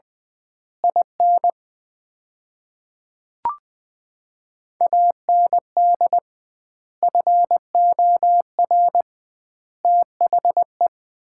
Morse code, heard as a keyed tone, about 20 wpm, 700 Hz. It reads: IN AND FOR THE
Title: 2x Word-Spacing - Sets of 5 Words - 20wpm